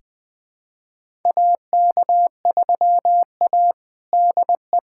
Morse code, heard as a keyed tone, about 21 wpm, 700 Hz. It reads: AK3A DE
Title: USA Callsign DE Callsign - 4 Characters - 20wpm